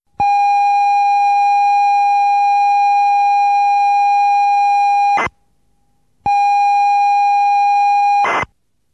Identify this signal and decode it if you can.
Computer data on cassette.